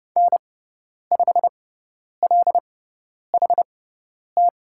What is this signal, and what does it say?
Morse code, heard as a keyed tone, about 30 wpm, spaced as for 10 wpm, 700 Hz. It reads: N5LHT
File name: Lesson 16g - Letter c - Call Signs 30wpm